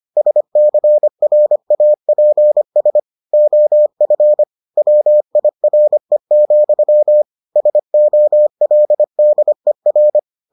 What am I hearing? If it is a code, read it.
Morse code, heard as a keyed tone, about 25 wpm, 600 Hz. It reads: SCRAPS OF WIRE, SOLDER